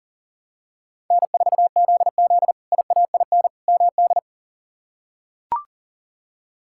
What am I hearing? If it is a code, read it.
Morse code, heard as a keyed tone, about 37 wpm, 700 Hz. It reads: N4ZZ IAIN MD
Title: K1USN SST QSO Practice - 40wpm